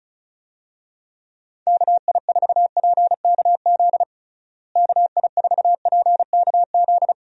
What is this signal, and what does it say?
Morse code, heard as a keyed tone, about 34 wpm, 700 Hz. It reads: KI4PKZ KI4PKZ
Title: USA Call Signs - 6 Characters - Repeated 3x - 35wpm